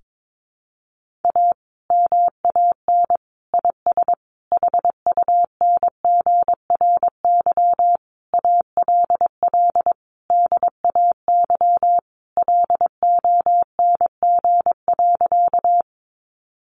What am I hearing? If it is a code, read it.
Morse code, heard as a keyed tone, about 22 wpm, 700 Hz. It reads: A MAN IS HUNGRY ALL DAY LONG.
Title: Sentences from Top 1400 Words 22wpm